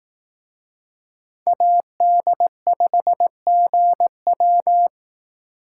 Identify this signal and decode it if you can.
Morse code, heard as a keyed tone, about 18 wpm, 700 Hz. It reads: AD5GW
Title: Speed Racing - USA 2x2 Call Signs 15wpm